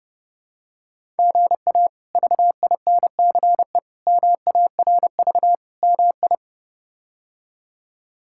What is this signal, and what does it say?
Morse code, heard as a keyed tone, about 32 wpm, 700 Hz. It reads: GA VINCE MARV MI
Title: K1USN SST QSO Practice - 30wpm